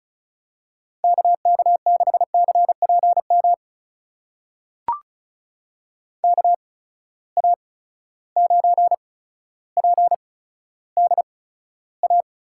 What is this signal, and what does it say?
Morse code, heard as a keyed tone, about 36 wpm, spaced as for 15 wpm, 700 Hz. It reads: KK6CPM KA9PDA
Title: USA Call Signs - 6 Characters - 8x Character Spacing - 35wpm